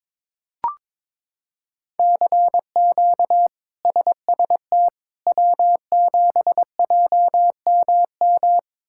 Morse code, heard as a keyed tone, about 22 wpm, 700 Hz. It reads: CQ SST W7JMM
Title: K1USN SST QSO Practice - 22wpm